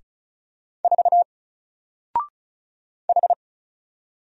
Morse code, heard as a keyed tone, about 36 wpm, 700 Hz. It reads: 4 H
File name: Single-Letter-Number-35wpm